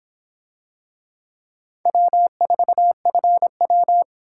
Morse code, heard as a keyed tone, about 26 wpm, 700 Hz. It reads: W4FW